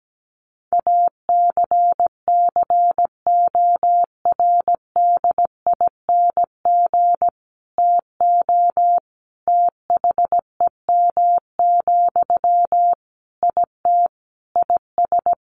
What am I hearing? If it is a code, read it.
Morse code, heard as a keyed tone, about 17 wpm, 700 Hz. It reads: ACCORDING TO THEM, IT IS
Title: Sentences from Top 800 Words 17wpm